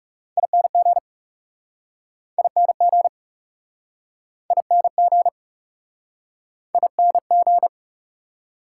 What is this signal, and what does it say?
Morse code, heard as a keyed tone, about 40 wpm, 690 Hz. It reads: ING ING ING ING